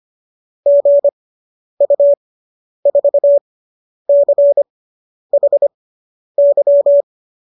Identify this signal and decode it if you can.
Morse code, heard as a keyed tone, about 25 wpm, spaced as for 10 wpm, 570 Hz. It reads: GU4CHY